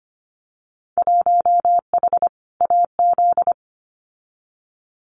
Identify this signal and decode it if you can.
Morse code, heard as a keyed tone, about 25 wpm, 700 Hz. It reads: 1H AZ
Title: Winter Field Day Exchanges 25wpm